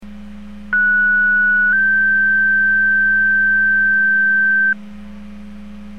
Two-tone page, heard as a one-second tone, then a three-second tone, about 1470 then 1550 Hz. It is heard from 0.7 to 4.7 s.